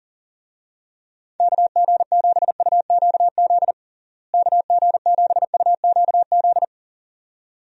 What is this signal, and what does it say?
Morse code, heard as a keyed tone, about 37 wpm, 700 Hz. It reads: KG7UQZ KG7UQZ
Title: USA Call Signs - 6 Characters - Repeated 2x - 40wpm